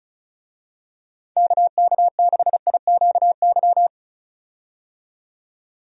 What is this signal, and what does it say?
Morse code, heard as a keyed tone, about 35 wpm, 700 Hz. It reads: KK6IQY
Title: USA Call Signs - 6 Characters - 35wpm